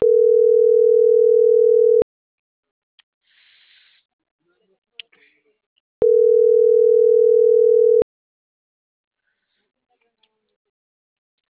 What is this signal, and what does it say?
Ringback tone (North America), 440 + 480 Hz.